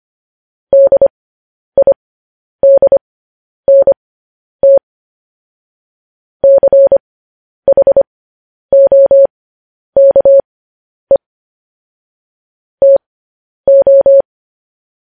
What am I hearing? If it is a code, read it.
Morse code, heard as a keyed tone, about 25 wpm, spaced as for 10 wpm, 550 Hz. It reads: DIDNT CHOKE TO
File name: podcast / News headlines-1x-25x10